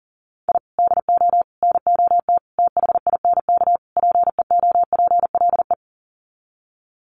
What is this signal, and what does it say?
Morse code, heard as a keyed tone, about 40 wpm, 700 Hz. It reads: I DO NOT THINK PEOPLE
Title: Sets of 5 Words Encore2 40wpm